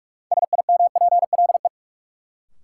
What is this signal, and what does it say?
Morse code, heard as a keyed tone, about 43 wpm, 700 Hz. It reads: SIMPLE
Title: Sentences from Top 400 Words - New Word Drill - 45wpm